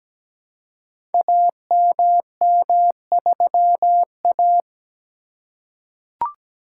Morse code, heard as a keyed tone, about 17 wpm, 700 Hz. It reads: AMM3A
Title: Code Groups - 5 Letter:Numbers - 17wpm